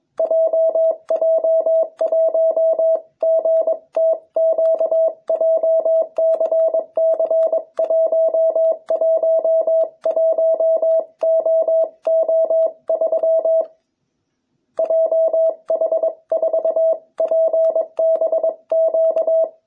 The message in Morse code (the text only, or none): JJ1GTQ JCC111OO3 JH4PBQ